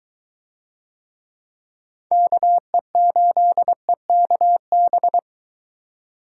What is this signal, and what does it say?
Morse code, heard as a keyed tone, about 22 wpm, 700 Hz. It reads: KE8EKB